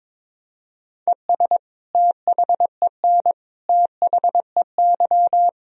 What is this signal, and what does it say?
Morse code, heard as a keyed tone, about 22 wpm, 700 Hz. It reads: ES THEN THEY
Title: Sentences from Top 300 Words and Abbreviations 22wpm